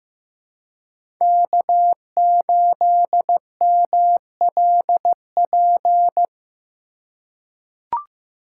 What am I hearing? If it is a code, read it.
Morse code, heard as a keyed tone, about 15 wpm, 700 Hz. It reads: K8MLP
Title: USA Call Signs - 5 Characters - 2x Character Spacing - 15wpm